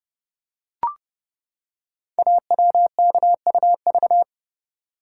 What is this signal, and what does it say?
Morse code, heard as a keyed tone, about 28 wpm, 700 Hz. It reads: AWKUV
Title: Code Groups - 5 Letters - 30wpm